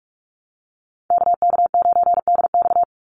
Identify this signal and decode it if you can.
Morse code, heard as a keyed tone, about 44 wpm, 700 Hz. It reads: KK9DX
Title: Call Signs Repeated 3x - 45wpm